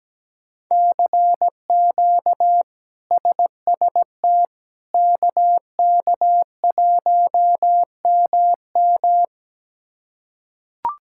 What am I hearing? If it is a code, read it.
Morse code, heard as a keyed tone, about 17 wpm, 700 Hz. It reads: CQ SST KK1MM
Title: K1USN SST QSO Practice - 17wpm